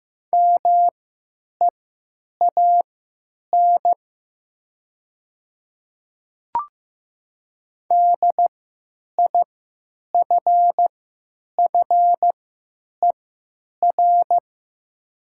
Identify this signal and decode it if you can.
Morse code, heard as a keyed tone, about 15 wpm, spaced as for 9 wpm, 700 Hz. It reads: MEAN DIFFER
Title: 3x Character Spacing - Top 200 Words - 15wpm